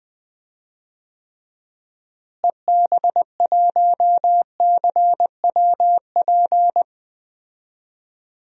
Morse code, heard as a keyed tone, about 21 wpm, 700 Hz. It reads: EB1CWP
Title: DXCC-100 Call Signs - 6 Characters - Repeated 3x - 20wpm